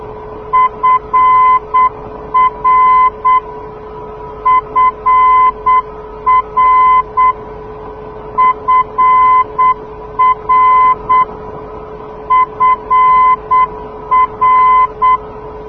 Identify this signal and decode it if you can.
Morse code, heard as a keyed tone, about 8 wpm, 1020 Hz. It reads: FR FR FR FR
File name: europe / d-6028 - ndb frankfurt fr-297 (2004) (rainer brannolte, d) xc